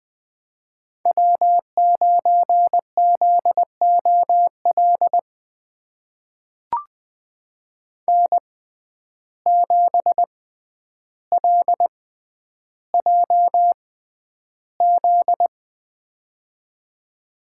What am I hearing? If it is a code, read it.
Morse code, heard as a keyed tone, about 20 wpm, spaced as for 10 wpm, 700 Hz. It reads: W9ZOL N7LJZ
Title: USA Call Signs - 6x Character Spacing - 20wpm